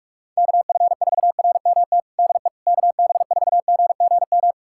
Morse code, heard as a keyed tone, about 46 wpm, 700 Hz. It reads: KF4RMT DE KB4GGM
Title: USA Callsign DE Callsign - 6 Characters - 45wpm